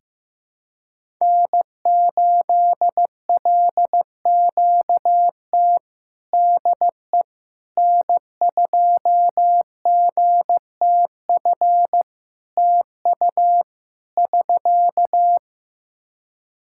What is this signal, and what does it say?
Morse code, heard as a keyed tone, about 15 wpm, 700 Hz. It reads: N8LQT DE N2GTF TU <SK>
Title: K1USN SST QSO Practice 15wpm